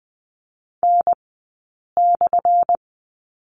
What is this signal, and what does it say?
Morse code, heard as a keyed tone, about 20 wpm, spaced as for 8 wpm, 700 Hz. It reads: N/
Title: 211 Lesson 34d - Number 8 - Two Characters 20wpm